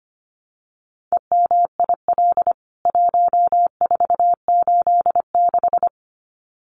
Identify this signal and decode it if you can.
Morse code, heard as a keyed tone, about 25 wpm, 690 Hz. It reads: EMIL 1486
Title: CWT Exchange 25wpm